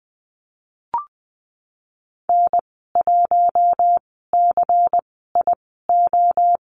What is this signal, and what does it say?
Morse code, heard as a keyed tone, about 20 wpm, spaced as for 15 wpm, 700 Hz. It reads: N1CIO
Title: 128 Lesson 21g - Number 6 - Call Signs 20wpm